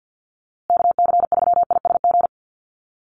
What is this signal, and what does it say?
Morse code, heard as a keyed tone, about 48 wpm, 700 Hz. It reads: KC3ISG
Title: USA Call Signs Repeated 3x 50wpm